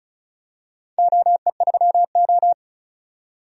Morse code, heard as a keyed tone, about 35 wpm, 700 Hz. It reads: OE3O
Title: DXCC-100 Call Signs - 4 Characters - 35wpm